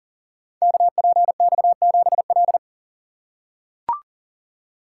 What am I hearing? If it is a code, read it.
Morse code, heard as a keyed tone, about 39 wpm, 700 Hz. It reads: KPX7L
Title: Code Groups - 5 Letter:Numbers - 40wpm